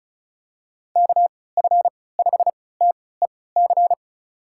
Morse code, heard as a keyed tone, about 34 wpm, spaced as for 20 wpm, 700 Hz. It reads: KF5TEC
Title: USA Call Signs - 6 Characters - 3x Character Spacing - 35wpm